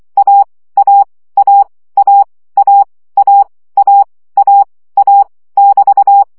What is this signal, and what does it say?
Morse code, heard as a keyed tone, about 24 wpm, 800 Hz. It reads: A A A A A A A A A <BT>
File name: CW gyakorlas a 20 baud